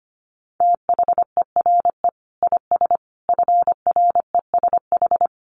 Morse code, heard as a keyed tone, about 25 wpm, 700 Hz. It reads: THERE IS FRESH
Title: Sentences from Top 1200 Words 25wpm